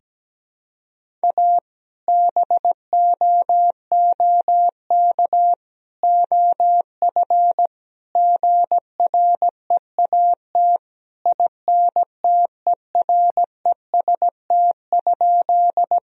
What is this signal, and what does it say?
Morse code, heard as a keyed tone, about 17 wpm, 700 Hz. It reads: A BOOK OF GREAT INTEREST?
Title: Sentences from Top 1000 Words - Review Part 2 - 17wpm